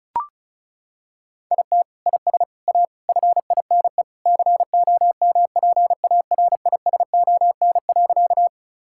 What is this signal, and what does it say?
Morse code, heard as a keyed tone, about 34 wpm, 700 Hz. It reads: IT IS A FINE COMPARISON.
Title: Sentences from Top 1600 Words 35wpm